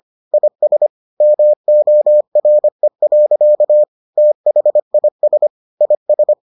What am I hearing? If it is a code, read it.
Morse code, heard as a keyed tone, about 25 wpm, 600 Hz. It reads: IS MORE. THIS IS